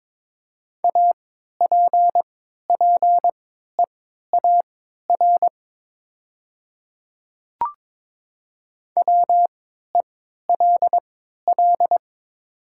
Morse code, heard as a keyed tone, about 22 wpm, spaced as for 13 wpm, 700 Hz. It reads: APPEAR WELL
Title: 3x Character Spacing - Top 500 Words - 22wpm